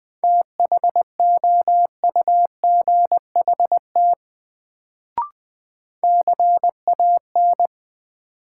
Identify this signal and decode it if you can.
Morse code, heard as a keyed tone, about 20 wpm, 700 Hz. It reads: THOUGHT CAN
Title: Sets of 2 Words - New Word Drill - 20wpm